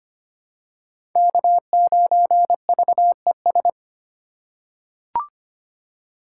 Morse code, heard as a keyed tone, about 25 wpm, 700 Hz. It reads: K9VES
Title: USA Call Signs - 5 Characters - 4x Character Spacing - 25wpm